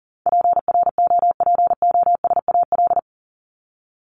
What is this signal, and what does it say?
Morse code, heard as a keyed tone, about 38 wpm, 700 Hz. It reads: PROPOSAL
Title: Top 1751-2000 Words 40wpm